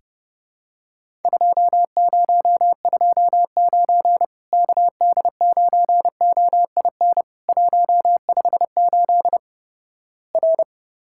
Morse code, heard as a keyed tone, about 30 wpm, 700 Hz. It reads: 2029 KD9OIN 158 R
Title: Worked All Europe DX Contest - QTC Exchange - 30wpm